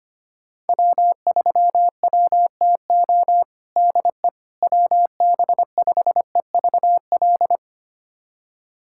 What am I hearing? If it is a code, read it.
Morse code, heard as a keyed tone, about 25 wpm, 700 Hz. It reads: W3WTO DE WB5EVL
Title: USA Callsign DE Callsign 25wpm